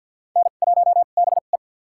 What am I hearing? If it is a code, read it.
Morse code, heard as a keyed tone, about 51 wpm, spaced as for 36 wpm, 700 Hz. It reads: N1BE
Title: USA Call Signs - 2x Character Spacing - 50wpm